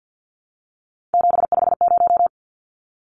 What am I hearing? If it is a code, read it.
Morse code, heard as a keyed tone, about 50 wpm, 700 Hz. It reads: 750
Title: Numbers - 3-Digit 50wpm